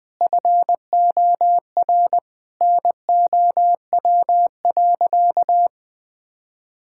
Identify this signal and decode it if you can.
Morse code, heard as a keyed tone, about 21 wpm, 700 Hz. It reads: FOR NOW.
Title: Sentences from Top 1000 Words - Review Part 5 - 20wpm